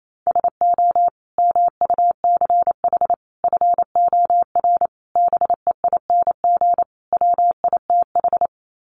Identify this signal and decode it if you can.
Morse code, heard as a keyed tone, about 28 wpm, 700 Hz. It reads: SO MUCH FOR BEING WITH
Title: Sets of 5 Words Encore 28wpm